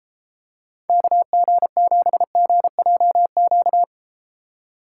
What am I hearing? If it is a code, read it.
Morse code, heard as a keyed tone, about 32 wpm, 700 Hz. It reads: KG7GJQ